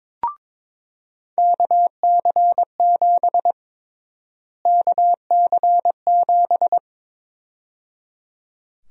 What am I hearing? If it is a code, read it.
Morse code, heard as a keyed tone, about 22 wpm, 700 Hz. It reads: KC7 KC7
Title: USA Call Sign Prefixes - Repeated 2x - 22wpm